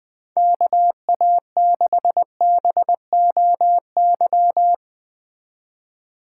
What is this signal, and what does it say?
Morse code, heard as a keyed tone, about 20 wpm, 700 Hz. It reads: KA6BOY
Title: Speed Racing - USA Call Signs 17wpm